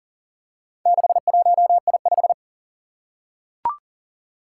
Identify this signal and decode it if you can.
Morse code, heard as a keyed tone, about 39 wpm, 700 Hz. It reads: 61I5
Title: Four Challenging Letter-Number Combinations 40wpm